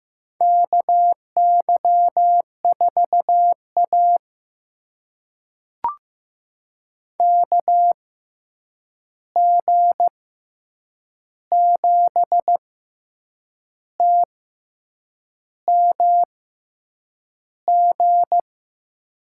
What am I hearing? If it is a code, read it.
Morse code, heard as a keyed tone, about 15 wpm, spaced as for 7 wpm, 700 Hz. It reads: KY4A KG7TMG
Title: USA Call Signs - 6x Character Spacing - 15wpm